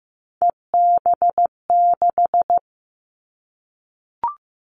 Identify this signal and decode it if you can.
Morse code, heard as a keyed tone, about 15 wpm, 700 Hz. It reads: EB6
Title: Callsign Prefixes for Top 100 DXCC - 15wpm